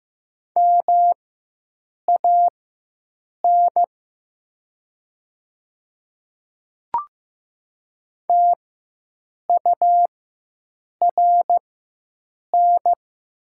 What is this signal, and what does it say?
Morse code, heard as a keyed tone, about 15 wpm, spaced as for 7 wpm, 700 Hz. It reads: MAN TURN
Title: 4x Character Spacing - Top 200 Words - 15wpm